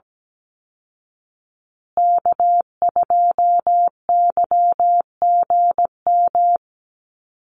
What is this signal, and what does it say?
Morse code, heard as a keyed tone, about 17 wpm, 700 Hz. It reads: K2YGM
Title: Call Signs Repeated 3x - 17wpm